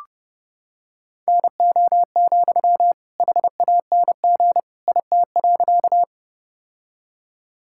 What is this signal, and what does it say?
Morse code, heard as a keyed tone, about 30 wpm, 700 Hz. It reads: NO, HANG IT.
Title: Sentences from Top 1600 Words 30wpm